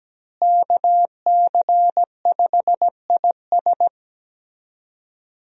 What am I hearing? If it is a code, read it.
Morse code, heard as a keyed tone, about 17 wpm, 700 Hz. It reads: KC5IS
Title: USA Call Signs - 5 Characters - Repeated 3x - 17wpm